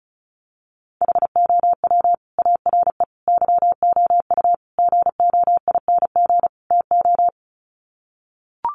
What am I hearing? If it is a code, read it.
Morse code, heard as a keyed tone, about 34 wpm, 700 Hz. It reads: HOW ARE YOU GOING TO
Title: Sets of 5 Words 35wpm